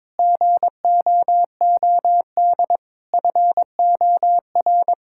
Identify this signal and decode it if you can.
Morse code, heard as a keyed tone, about 22 wpm, 700 Hz. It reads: GOOD FOR
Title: Sentences from Top 1000 Words - Review Part 6 - 22wpm